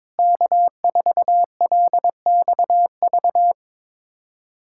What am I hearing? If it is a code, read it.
Morse code, heard as a keyed tone, about 22 wpm, 700 Hz. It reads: K4LXV
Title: USA Callsign DE Callsign 22wpm